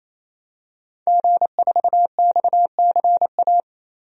Morse code, heard as a keyed tone, about 27 wpm, 700 Hz. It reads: G4XCA
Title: DXCC-100 Call Signs - 5 Characters - 28wpm